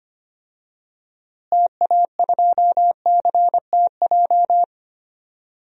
Morse code, heard as a keyed tone, about 25 wpm, 700 Hz. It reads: TA2CTJ
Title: DXCC-100 Call Signs - 6 Characters - Repeated 2x - 25wpm